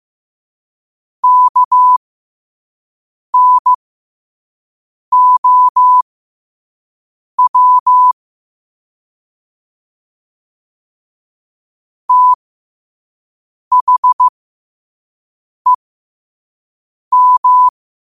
Morse code, heard as a keyed tone, about 15 wpm, spaced as for 5 wpm, 1000 Hz. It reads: KNOW THEM
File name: qotdfiles / 5wpm